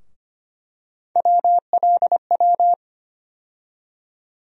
Morse code, heard as a keyed tone, about 25 wpm, 700 Hz. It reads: WLW
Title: USA Call Sign Postfixes - 3 Letters 25wpm